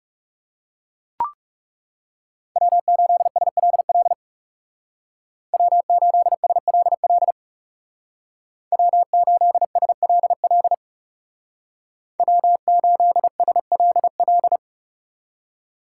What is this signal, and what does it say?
Morse code, heard as a keyed tone, about 37 wpm, 700 Hz. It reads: W8SLL W8SLL W8SLL W8SLL
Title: Speed Racing - USA 1x3 Call Signs 30wpm